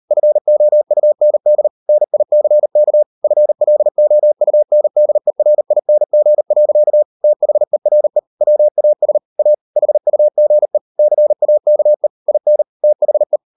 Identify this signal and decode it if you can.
Morse code, heard as a keyed tone, about 38 wpm, 600 Hz. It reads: FOUND NICK FLOUNDERING. THERE WAS A HUGE CAKE IN THE